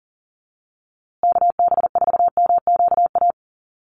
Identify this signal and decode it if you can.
Morse code, heard as a keyed tone, about 40 wpm, 700 Hz. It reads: KB4MQA
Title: USA Call Signs Repeated 3x 40wpm